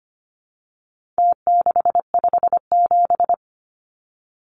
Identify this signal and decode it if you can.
Morse code, heard as a keyed tone, about 25 wpm, 700 Hz. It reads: T657
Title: Numbers - Contest Style - 4-Digit 25wpm